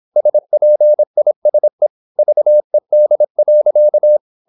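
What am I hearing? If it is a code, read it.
Morse code, heard as a keyed tone, about 26 wpm, 600 Hz. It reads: SPISE VED.
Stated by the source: High-speed (QRQ) Morse at 26 wpm.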